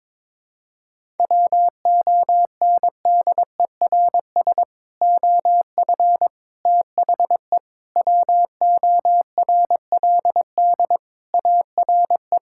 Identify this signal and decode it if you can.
Morse code, heard as a keyed tone, about 22 wpm, 700 Hz. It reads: WONDERS OF THE WORLD ARE
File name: practice / 22wpm / Sentences with 3-Word Phrases - 22wpm